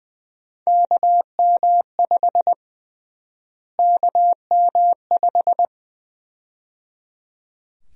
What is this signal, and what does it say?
Morse code, heard as a keyed tone, about 20 wpm, 700 Hz. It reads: KM5 KM5
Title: USA Call Sign Prefixes - Repeated 2x - 20wpm